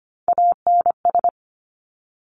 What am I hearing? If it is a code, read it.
Morse code, heard as a keyed tone, about 25 wpm, 700 Hz. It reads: ANS
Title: Top Three Letter Combinations in Words 25wpm